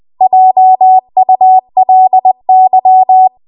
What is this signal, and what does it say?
Morse code, heard as a keyed tone, about 20 wpm, 750 Hz. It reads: JULY
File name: Archive / 20 WPM / 20WPM